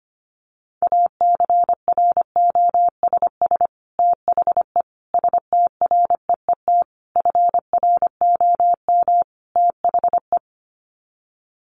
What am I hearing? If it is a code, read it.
Morse code, heard as a keyed tone, about 25 wpm, 700 Hz. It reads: ACROSS THE STREET FROM THE
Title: Sets of 5 Words Encore 25wpm